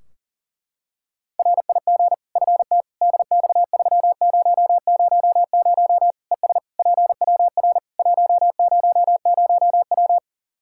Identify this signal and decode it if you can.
Morse code, heard as a keyed tone, about 40 wpm, 700 Hz. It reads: RIG FT DX3000 ES PWR 100W
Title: K1USN SST QSO Practice 40wpm